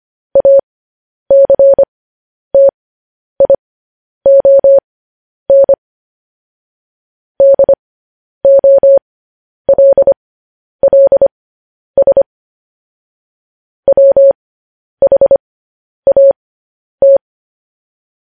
Morse code, heard as a keyed tone, about 25 wpm, spaced as for 10 wpm, 550 Hz. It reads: ACTION DOLLS WHAT
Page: news headlines in morse code at 10 wpm